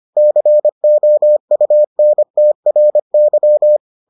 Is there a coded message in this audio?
Yes: Morse code from 0.2 to 3.8 s, heard as a keyed tone, about 25 wpm, 600 Hz. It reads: COUNTRY